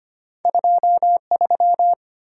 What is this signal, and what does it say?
Morse code, heard as a keyed tone, about 25 wpm, 700 Hz. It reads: 23
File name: Two Letter-Number Combinations - 25wpm